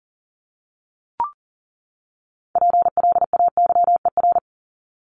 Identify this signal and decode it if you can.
Morse code, heard as a keyed tone, about 40 wpm, 700 Hz. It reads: PLAYER
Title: Top 501-750 Words 40wpm